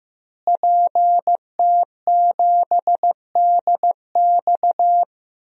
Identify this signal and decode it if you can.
Morse code, heard as a keyed tone, about 15 wpm, 700 Hz. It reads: PT7DX
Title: DXCC-100 Call Signs - 5 Characters - Repeated 3x - 15wpm